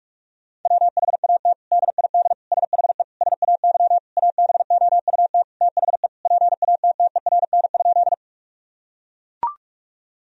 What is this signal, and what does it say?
Morse code, heard as a keyed tone, about 41 wpm, 700 Hz. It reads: WHAT DID SHE SAY ABOUT THE PATTERN?
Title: Sentences from Top 1000 Words - Review Part 2 - 45wpm